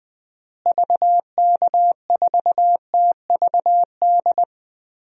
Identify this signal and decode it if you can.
Morse code, heard as a keyed tone, about 20 wpm, 700 Hz. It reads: VK4TVD
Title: DXCC-100 Call Signs - 6 Characters - Repeated 2x - 20wpm